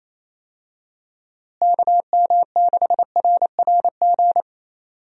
Morse code, heard as a keyed tone, about 28 wpm, 700 Hz. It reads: KM6RRG
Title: USA Call Signs - 6 Characters - Repeated 2x - 28wpm